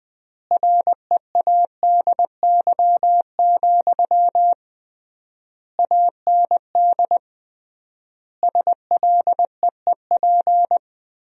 Morse code, heard as a keyed tone, about 20 wpm, 700 Hz. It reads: READY, AND SLEEP
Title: 3x Word-Spacing - Sentences from Top 500 Words - 20wpm